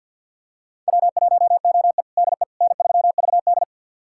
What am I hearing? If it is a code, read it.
Morse code, heard as a keyed tone, about 50 wpm, 690 Hz. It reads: W1OE DE N3VD
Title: USA Callsign DE Callsign - 4 Characters - 50wpm